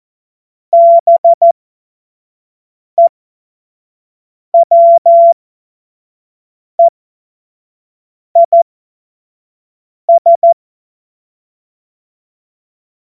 Morse code, heard as a keyed tone, about 14 wpm, spaced as for 5 wpm, 680 Hz. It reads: BEWEIS
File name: Morsekurs / Witze / Witz 05. April 2022 25 F70